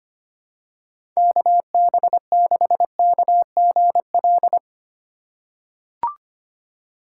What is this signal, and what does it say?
Morse code, heard as a keyed tone, about 25 wpm, 700 Hz. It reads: KB6KGL
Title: USA Call Signs - 6 Characters - 2x Character Spacing - 25wpm